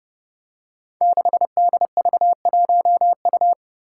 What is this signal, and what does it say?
Morse code, heard as a keyed tone, about 32 wpm, 700 Hz. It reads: 6DV1U
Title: Five Challenging Letter-Number Combinations 30wpm